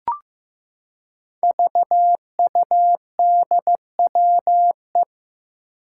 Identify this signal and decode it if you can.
Morse code, heard as a keyed tone, about 15 wpm, 700 Hz. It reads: VUDWE
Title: Five Challenging Letter-Number Combinations 15wpm